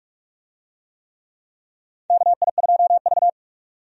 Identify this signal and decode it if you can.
Morse code, heard as a keyed tone, about 46 wpm, 700 Hz. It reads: KI2V
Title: USA Call Signs - 4 Characters - Repeated 3x - 45wpm